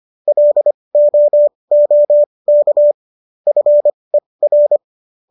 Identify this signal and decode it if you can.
Morse code, heard as a keyed tone, about 25 wpm, spaced as for 20 wpm, 590 Hz. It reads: LOOK FER